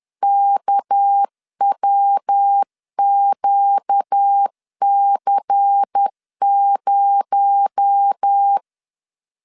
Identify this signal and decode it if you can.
Morse code, heard as a keyed tone, about 10 wpm, 800 Hz. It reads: KWQC0